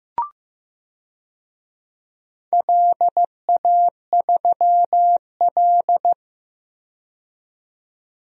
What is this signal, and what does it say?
Morse code, heard as a keyed tone, about 15 wpm, 700 Hz. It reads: LA3L
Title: DXCC-100 Call Signs - 4 Characters - 15wpm